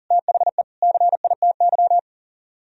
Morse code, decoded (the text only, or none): THE CITY